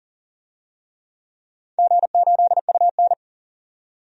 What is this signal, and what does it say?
Morse code, heard as a keyed tone, about 38 wpm, 700 Hz. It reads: G8UN
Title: DXCC-100 Call Signs - 4 Characters - 40wpm